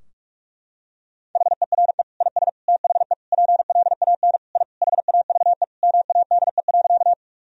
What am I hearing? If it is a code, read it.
Morse code, heard as a keyed tone, about 41 wpm, 700 Hz. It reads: HERE IS THE PLAN I HAVE MADE.
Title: Sentences from Top 1000 Words - Review Part 2 - 45wpm